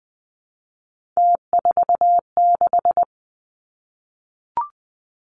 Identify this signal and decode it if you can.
Morse code, heard as a keyed tone, about 20 wpm, 700 Hz. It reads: T46
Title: Three Challenging Letter-Number Combinations 20wpm